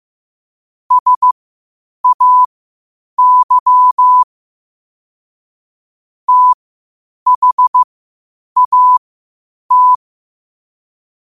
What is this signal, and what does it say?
Morse code, heard as a keyed tone, about 15 wpm, spaced as for 8 wpm, 1000 Hz. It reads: SAY THAT
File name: qotdfiles / 8wpm